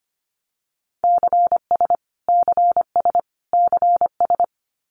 Morse code, heard as a keyed tone, about 25 wpm, 700 Hz. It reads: CS CS CS
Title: Abbreviations Repeated 3x - 25wpm